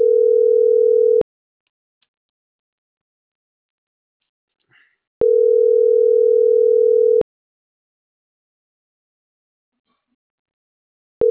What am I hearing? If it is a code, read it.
Ringback tone (North America), 440 + 480 Hz.